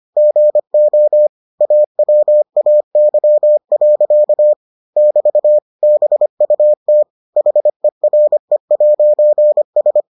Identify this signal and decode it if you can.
Morse code, heard as a keyed tone, about 25 wpm, 600 Hz. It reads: GO AWAY. <BT> BUT HERE'S